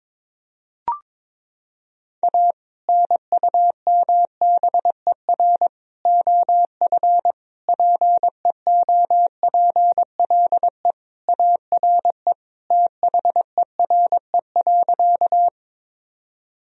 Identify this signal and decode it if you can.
Morse code, heard as a keyed tone, about 22 wpm, 700 Hz. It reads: A NUMBER OF PEOPLE ARE THERE.
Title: Sentences from Top 1000 Words - Review Part 1 - 22wpm